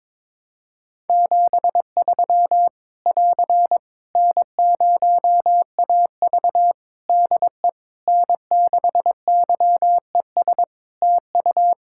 Morse code, heard as a keyed tone, about 22 wpm, 700 Hz. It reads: 73 <AR> N0AV DE N6YES TU
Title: Practice QSO - Exchange 4 - Ending - 22wpm